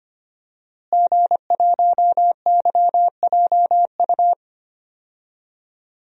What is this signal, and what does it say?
Morse code, heard as a keyed tone, about 25 wpm, 700 Hz. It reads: G1YJU
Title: DXCC-100 Call Signs - 5 Characters - Repeated 3x - 25wpm